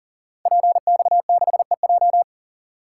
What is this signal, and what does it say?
Morse code, heard as a keyed tone, about 42 wpm, 690 Hz. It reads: PX6EJ
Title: Code Groups - 5 Letter:Numbers - 40wpm